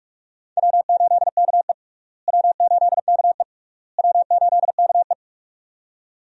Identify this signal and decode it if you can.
Morse code, heard as a keyed tone, about 46 wpm, 700 Hz. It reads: W8KE W8KE W8KE
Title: USA Call Signs - 4 Characters - Repeated 3x - 45wpm